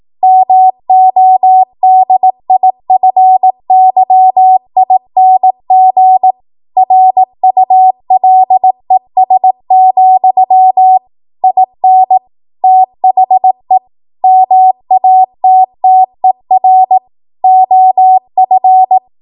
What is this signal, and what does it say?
Morse code, heard as a keyed tone, about 18 wpm, 750 Hz. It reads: MODIFYING RULES, IN THE MATTER OF
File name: Morse / 18WPM